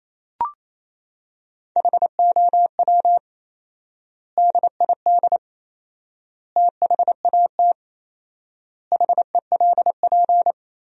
Morse code, heard as a keyed tone, about 28 wpm, 700 Hz. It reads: HOW DID THAT HELP
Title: 4x Word-Spacing - Sentences from Top 200 Words - 28wpm